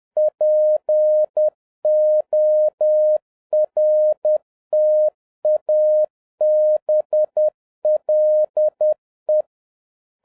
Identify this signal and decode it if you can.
Morse code, heard as a keyed tone, about 10 wpm, 610 Hz. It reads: PORTABLE
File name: 2011-10-mp3 / n11-MEETING WITH GOOD FRIENDS 10wpm